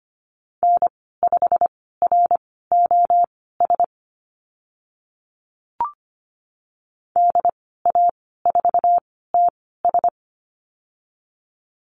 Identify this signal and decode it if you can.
Morse code, heard as a keyed tone, about 25 wpm, spaced as for 16 wpm, 700 Hz. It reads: N5ROS DA4TS
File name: Lesson 15g - Number 5 - Call Signs 25wpm